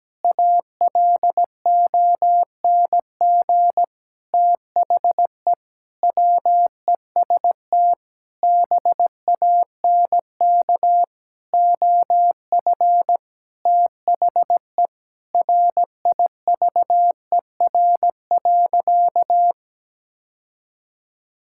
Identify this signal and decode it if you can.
Morse code, heard as a keyed tone, about 17 wpm, 700 Hz. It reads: ALONG THE WEST BANK OF THE RIVER.
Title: Sentences from Top 1000 Words - Review Part 6 - 17wpm